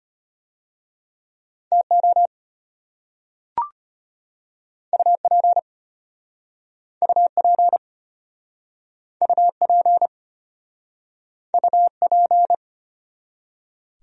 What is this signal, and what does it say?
Morse code, heard as a keyed tone, about 35 wpm, 700 Hz. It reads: TO UP UP UP UP